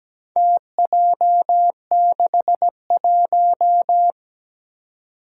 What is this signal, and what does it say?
Morse code, heard as a keyed tone, about 17 wpm, 700 Hz. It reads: TJ61
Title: Four Challenging Letter-Number Combinations 17wpm